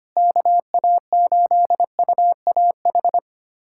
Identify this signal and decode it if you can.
Morse code, heard as a keyed tone, about 25 wpm, 700 Hz. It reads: KA8UAH
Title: USA Callsign DE Callsign 25wpm